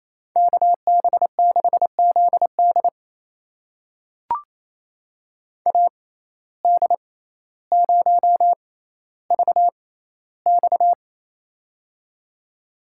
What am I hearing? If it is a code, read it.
Morse code, heard as a keyed tone, about 28 wpm, spaced as for 14 wpm, 700 Hz. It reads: KB6ZD AD0VX
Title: USA Call Signs - 5 Characters - 6x Character Spacing - 28wpm